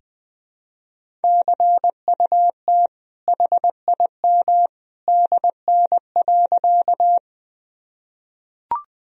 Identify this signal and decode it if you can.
Morse code, heard as a keyed tone, about 20 wpm, 700 Hz. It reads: CUT HIM DN.
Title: Sentences from Top 300 Words and Abbreviations 20wpm